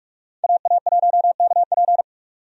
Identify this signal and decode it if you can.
Morse code, heard as a keyed tone, about 46 wpm, 700 Hz. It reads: AA1KP
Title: USA Call Signs - 5 Characters - 45wpm